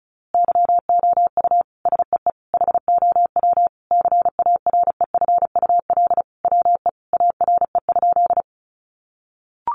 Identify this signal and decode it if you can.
Morse code, heard as a keyed tone, about 34 wpm, 700 Hz. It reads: YOU SEE HOW CAREFUL WE ARE?
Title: A Sound Of Thunder 35wpm